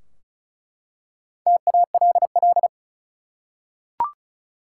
Morse code, heard as a keyed tone, about 36 wpm, 700 Hz. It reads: TALL
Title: Four Letter Words - 35wpm